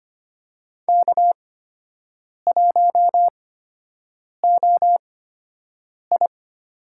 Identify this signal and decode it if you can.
Morse code, heard as a keyed tone, about 25 wpm, spaced as for 7 wpm, 700 Hz. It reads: K1OI
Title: USA Call Signs - 4 Characters - 8x Character Spacing - 25wpm